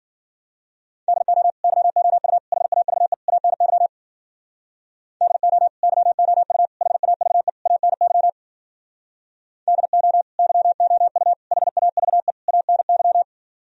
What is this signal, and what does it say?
Morse code, heard as a keyed tone, about 52 wpm, 700 Hz. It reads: DO YOU HAVE ANY DO YOU HAVE ANY DO YOU HAVE ANY